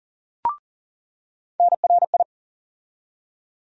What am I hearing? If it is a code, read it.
Morse code, heard as a keyed tone, about 40 wpm, 700 Hz. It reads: NRI
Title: USA Call Sign Postfixes - 3 Letters 40wpm